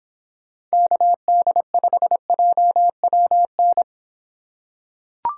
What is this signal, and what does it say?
Morse code, heard as a keyed tone, about 25 wpm, 700 Hz. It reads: KD5JWN